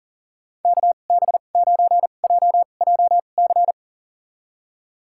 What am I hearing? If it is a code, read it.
Morse code, heard as a keyed tone, about 38 wpm, spaced as for 28 wpm, 700 Hz. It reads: KB9JJC
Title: USA Call Signs - 6 Characters - 2x Character Spacing - 40wpm